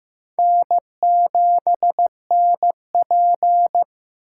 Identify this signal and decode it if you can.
Morse code, heard as a keyed tone, about 15 wpm, 700 Hz. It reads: N7NP
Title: USA Call Signs - 4 Characters - 4x Character Spacing - 15wpm